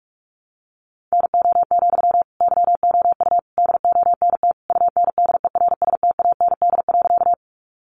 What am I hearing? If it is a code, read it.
Morse code, heard as a keyed tone, about 44 wpm, 700 Hz. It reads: NO, YOU DONT UNDERSTAND.
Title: Sentences from Top 600 Words 45wpm